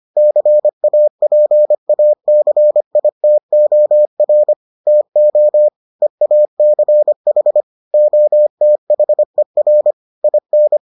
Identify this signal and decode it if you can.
Morse code, heard as a keyed tone, about 25 wpm, 600 Hz. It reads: CAPACITOR TO EACH OTHER IN